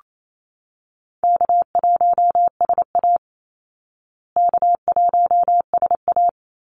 Morse code, heard as a keyed tone, about 28 wpm, 700 Hz. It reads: K1SA K1SA
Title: Easy Call Signs Repeated 2x - 28wpm